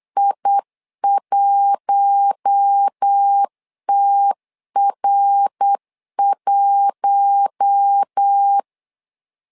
Morse code, heard as a keyed tone, about 8 wpm, 800 Hz. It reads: I1TR1
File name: LEZIONI / 4 Q1 8wpm 3035